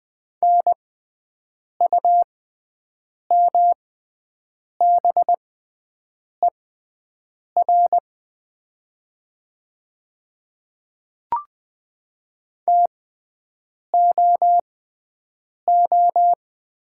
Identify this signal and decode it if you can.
Morse code, heard as a keyed tone, about 20 wpm, spaced as for 7 wpm, 700 Hz. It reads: NUMBER TOO
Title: 6x Character Spacing - Top 200 Words - 20wpm